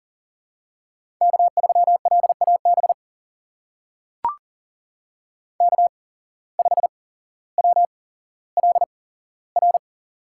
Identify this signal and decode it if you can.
Morse code, heard as a keyed tone, about 41 wpm, spaced as for 17 wpm, 700 Hz. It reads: K3LAB K5WLR
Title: USA Call Signs - 5 Characters - 8x Character Spacing - 40wpm